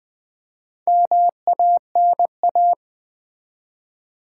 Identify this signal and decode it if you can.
Morse code, heard as a keyed tone, about 20 wpm, 700 Hz. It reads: MANA
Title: Build A Word - 2 Letters At A Time - 6-Letter Words - 20wpm